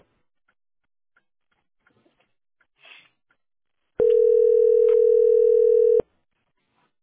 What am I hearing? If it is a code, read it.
Ringback tone (North America), 440 + 480 Hz.